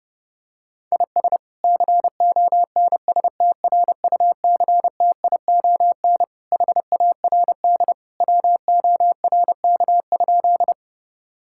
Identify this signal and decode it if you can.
Morse code, heard as a keyed tone, about 32 wpm, 700 Hz. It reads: IS CONSTRUCTION HARD WORK?
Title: Sentences from Top 1600 Words 30wpm